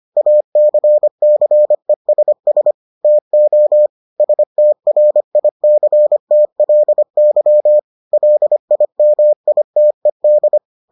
Morse code, heard as a keyed tone, about 25 wpm, 600 Hz. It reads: ACCESS TO STRICTLY LIMITED